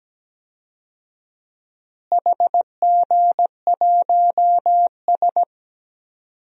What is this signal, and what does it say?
Morse code, heard as a keyed tone, about 17 wpm, 700 Hz. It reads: HG1S
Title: DXCC-100 Call Signs - 4 Characters - Repeated 2x - 17wpm